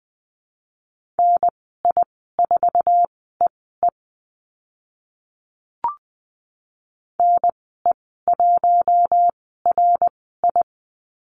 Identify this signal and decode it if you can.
Morse code, heard as a keyed tone, about 20 wpm, spaced as for 15 wpm, 700 Hz. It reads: NI4EE NE1RI
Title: 069 Lesson 12g - Letter d - Call Signs 20wpm